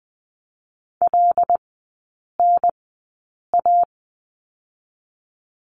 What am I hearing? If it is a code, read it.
Morse code, heard as a keyed tone, about 20 wpm, spaced as for 8 wpm, 700 Hz. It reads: LNA
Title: 087 Lesson 15e - Number 5 - Three Characters 20wpm